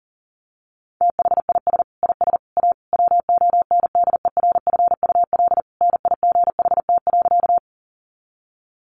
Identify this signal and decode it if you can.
Morse code, heard as a keyed tone, about 40 wpm, 700 Hz. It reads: THIS IS A WONDERFUL NIGHT.
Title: Sentences from Top 1100 Words 40wpm